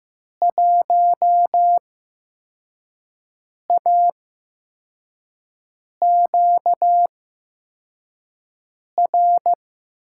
Morse code, heard as a keyed tone, about 15 wpm, spaced as for 4 wpm, 700 Hz. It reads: 1AQR
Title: USA Call Signs - 8x Character Spacing - 15wpm